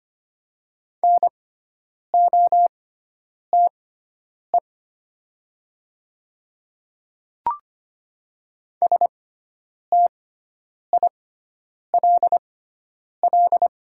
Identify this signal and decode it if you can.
Morse code, heard as a keyed tone, about 25 wpm, spaced as for 9 wpm, 690 Hz. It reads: NOTE STILL
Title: 6x Character Spacing - Top 500 Words - 25wpm